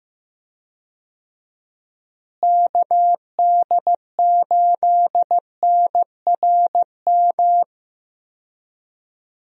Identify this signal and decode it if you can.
Morse code, heard as a keyed tone, about 15 wpm, 700 Hz. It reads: KD8NRM
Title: USA Call Signs - 6 Characters - Repeated 2x - 15wpm